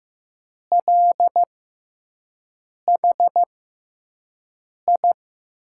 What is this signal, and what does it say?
Morse code, heard as a keyed tone, about 15 wpm, spaced as for 5 wpm, 700 Hz. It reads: LHI
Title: USA Call Signs - 6 Characters - 6x Character Spacing - 15wpm